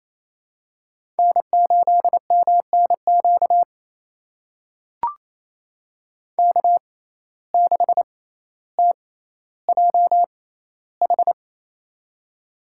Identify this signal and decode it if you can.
Morse code, heard as a keyed tone, about 27 wpm, spaced as for 14 wpm, 700 Hz. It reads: N8MNQ K6TJH
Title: USA Call Signs - 6x Character Spacing - 28wpm